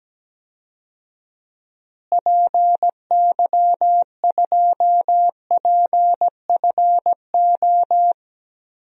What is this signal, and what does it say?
Morse code, heard as a keyed tone, about 17 wpm, 700 Hz. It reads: PY2PFO
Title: DXCC-100 Call Signs - 6 Characters - Repeated 2x - 17wpm